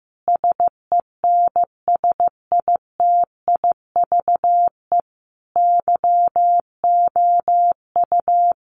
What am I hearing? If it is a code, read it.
Morse code, heard as a keyed tone, about 15 wpm, 700 Hz. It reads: SENSITIVE YOU
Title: Sentences from Top 1400 Words 15wpm